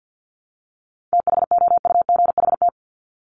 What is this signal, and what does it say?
Morse code, heard as a keyed tone, about 50 wpm, 700 Hz. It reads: THOUGHT
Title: Top 500 Words 50wpm